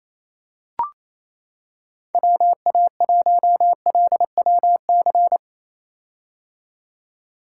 Morse code, heard as a keyed tone, about 28 wpm, 700 Hz. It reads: WA1LWC
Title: K1USN SST QSO Practice - 28wpm